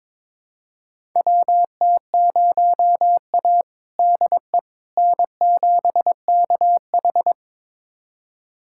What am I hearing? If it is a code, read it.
Morse code, heard as a keyed tone, about 22 wpm, 700 Hz. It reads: WT0A DE N7KH
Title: USA Callsign DE Callsign - 4 Characters - 22wpm